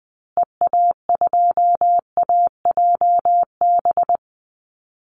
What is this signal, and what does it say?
Morse code, heard as a keyed tone, about 20 wpm, 700 Hz. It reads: EA2AJB
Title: Call Signs Repeated 2x - 20wpm